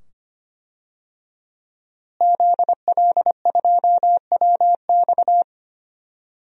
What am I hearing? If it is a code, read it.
Morse code, heard as a keyed tone, about 25 wpm, 700 Hz. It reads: ZL2WX
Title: DXCC-100 Call Signs - 5 Characters - Repeated 2x - 25wpm